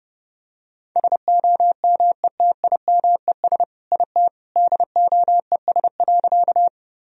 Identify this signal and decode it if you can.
Morse code, heard as a keyed tone, about 30 wpm, 700 Hz. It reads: SOMETIMES IT DOES.
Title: Sentences from Top 1000 Words - Review Part 3 - 30wpm